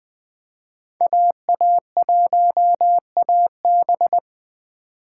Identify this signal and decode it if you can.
Morse code, heard as a keyed tone, about 20 wpm, 700 Hz. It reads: AA1AB